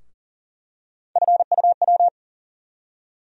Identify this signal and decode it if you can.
Morse code, heard as a keyed tone, about 40 wpm, 700 Hz. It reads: FUW
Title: USA Call Sign Postfixes - 3 Letters 40wpm